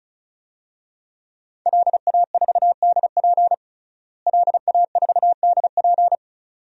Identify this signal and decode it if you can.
Morse code, heard as a keyed tone, about 36 wpm, 700 Hz. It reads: LA4DP LA4DP
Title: DXCC-100 Call Signs - 5 Characters - Repeated 2x - 35wpm